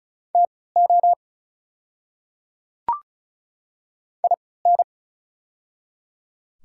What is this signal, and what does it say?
Morse code, heard as a keyed tone, about 34 wpm, spaced as for 20 wpm, 690 Hz. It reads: TO IN
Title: 3x Character Spacing - Top 100 Words - 35wpm